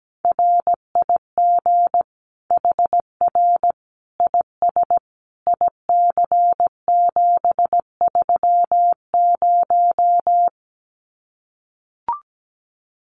Morse code, heard as a keyed tone, about 17 wpm, 700 Hz. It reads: RIG HR IS IC730
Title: Common QSO Phrases 17wpm